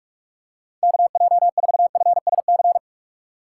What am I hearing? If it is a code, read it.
Morse code, heard as a keyed tone, about 44 wpm, 700 Hz. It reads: KJ4FSC